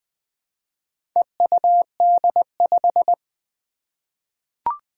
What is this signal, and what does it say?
Morse code, heard as a keyed tone, about 20 wpm, 700 Hz. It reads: EUD5
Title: Four Challenging Letter-Number Combinations 20wpm